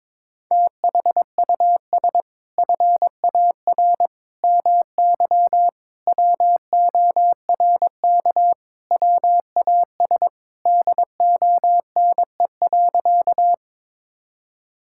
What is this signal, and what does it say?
Morse code, heard as a keyed tone, about 22 wpm, 700 Hz. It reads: THUS FAR MY WORK WAS DONE.
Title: Sentences from Top 1000 Words - Review Part 5 - 22wpm